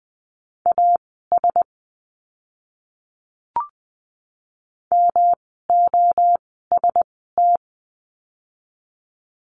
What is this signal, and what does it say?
Morse code, heard as a keyed tone, about 20 wpm, spaced as for 14 wpm, 700 Hz. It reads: AS MOST
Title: 139 Lesson 23f - Letter f - Words 20wpm